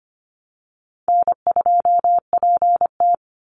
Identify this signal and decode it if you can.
Morse code, heard as a keyed tone, about 25 wpm, 700 Hz. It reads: N2PT
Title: USA Call Signs 25wpm